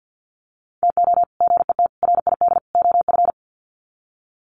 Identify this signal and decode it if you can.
Morse code, heard as a keyed tone, about 50 wpm, 700 Hz. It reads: TO GET RID OF
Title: Sets of 4 Words 50wpm